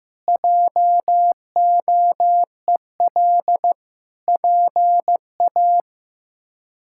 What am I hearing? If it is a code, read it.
Morse code, heard as a keyed tone, about 15 wpm, 700 Hz. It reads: JOEL PA
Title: K1USN SST QSO Practice - 15wpm